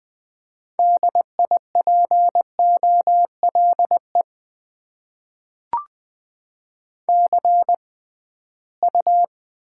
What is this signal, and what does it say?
Morse code, heard as a keyed tone, about 20 wpm, spaced as for 15 wpm, 700 Hz. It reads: DIPOLE CU
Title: Top 90 QSO Elements - 6x Character Spacing - 20wpm